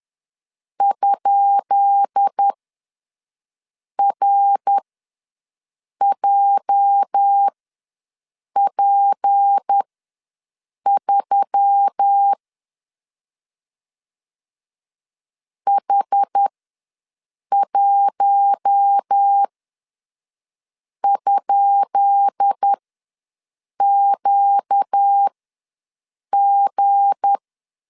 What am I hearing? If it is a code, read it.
Morse code, heard as a keyed tone, about 10 wpm, spaced as for 5 wpm, 800 Hz. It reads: ?RJP3 H1?QG
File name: LEZIONI / 10 SIGNS / 10B INT ws 10wpm 4134